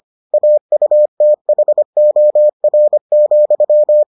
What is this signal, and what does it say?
Morse code, heard as a keyed tone, about 25 wpm, 600 Hz. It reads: AUTHOR,